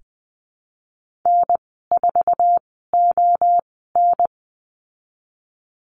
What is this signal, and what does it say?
Morse code, heard as a keyed tone, about 20 wpm, spaced as for 14 wpm, 700 Hz. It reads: N4ON
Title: 076 Lesson 13g - Letter L - Call Signs 20wpm